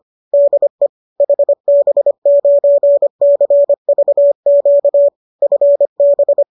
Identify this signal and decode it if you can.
Morse code, heard as a keyed tone, about 25 wpm, 570 Hz. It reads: DE HB9CVQ FB